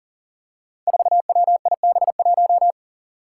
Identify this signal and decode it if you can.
Morse code, heard as a keyed tone, about 40 wpm, 700 Hz. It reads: 4WIB1
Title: Five Challenging Letter-Number Combinations 40wpm